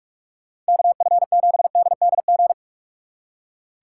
Morse code, heard as a keyed tone, about 59 wpm, spaced as for 52 wpm, 700 Hz. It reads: KF7DDG
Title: Speed Racing - USA 2x3 Call Signs 45wpm